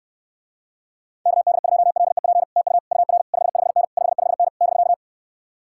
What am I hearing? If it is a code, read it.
Morse code, heard as a keyed tone, about 69 wpm, 700 Hz. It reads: KD2LEP TU UR 55N 55N <BK>